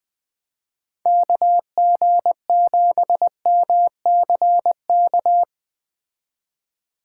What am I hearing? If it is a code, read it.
Morse code, heard as a keyed tone, about 21 wpm, 700 Hz. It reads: KG7MCK
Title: Speed Racing - USA 2x3 Call Signs 17wpm